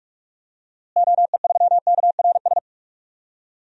Morse code, heard as a keyed tone, about 45 wpm, 700 Hz. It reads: OE3KRS
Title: DXCC-100 Call Signs - 6 Characters - Repeated 2x - 45wpm